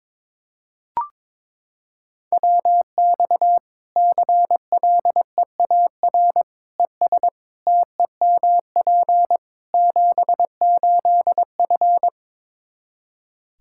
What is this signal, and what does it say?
Morse code, heard as a keyed tone, about 22 wpm, 700 Hz. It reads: WX CLEAR ES TEMP 78F
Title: K1USN SST QSO Practice 22wpm